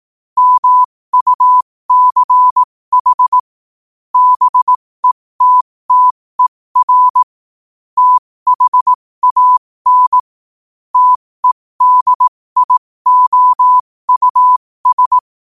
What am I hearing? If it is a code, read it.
Morse code, heard as a keyed tone, about 18 wpm, spaced as for 15 wpm, 1000 Hz. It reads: MUCH BETTER THAN TEDIOUS
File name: qotdfiles / 15wpm